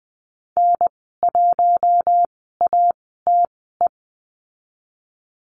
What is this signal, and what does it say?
Morse code, heard as a keyed tone, about 20 wpm, spaced as for 14 wpm, 700 Hz. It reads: N1ATE
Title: 048 Lesson 9g - Number 4 - Call Signs 20wpm